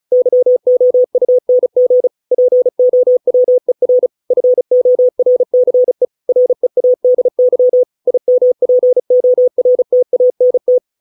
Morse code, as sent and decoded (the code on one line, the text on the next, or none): -.-- --- ..- -. --. / .--. --- .-- . .-. / ..-. --- .-. -.-. / .-. .- -.. -.-- / .. -- .--. --- .-. - .- -. -
YOUNG POWER FORCE READY IMPORTANT